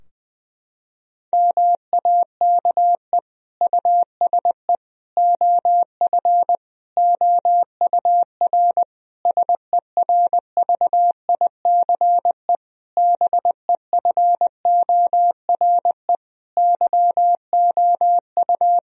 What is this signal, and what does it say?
Morse code, heard as a keyed tone, about 21 wpm, 700 Hz. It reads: MAKE USE OF OUR SERVICE BEFORE YOU
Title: Sentences from Top 1000 Words - Review Part 4 - 20wpm